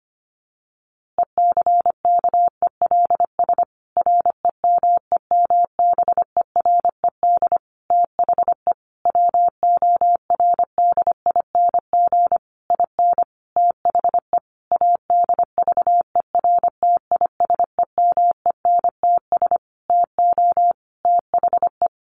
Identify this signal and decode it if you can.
Morse code, heard as a keyed tone, about 25 wpm, 700 Hz. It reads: ECKELS REMEMBERED THE WORDING IN THE ADVERTISEMENTS TO THE
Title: A Sound Of Thunder 25wpm